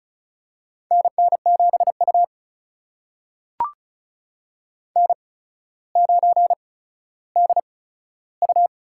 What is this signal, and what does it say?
Morse code, heard as a keyed tone, about 36 wpm, spaced as for 15 wpm, 700 Hz. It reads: NN7U N9DU
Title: USA Call Signs - 4 Characters - 8x Character Spacing - 35wpm